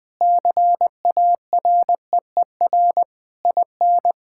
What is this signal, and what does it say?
Morse code, heard as a keyed tone, about 21 wpm, 700 Hz. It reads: CAREER IN
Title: Sentences from Top 1000 Words - Review Part 6 - 20wpm